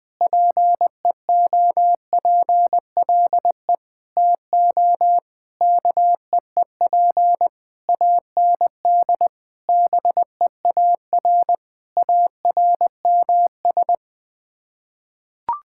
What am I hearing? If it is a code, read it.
Morse code, heard as a keyed tone, about 20 wpm, 700 Hz. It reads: PEOPLE TO KEEP AND BEAR ARMS
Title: Sets of 9 Words - 20wpm